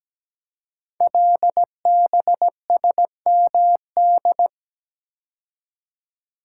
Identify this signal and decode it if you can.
Morse code, heard as a keyed tone, about 17 wpm, 700 Hz. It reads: LBSMD
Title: Code Groups - 5 Letters - 17wpm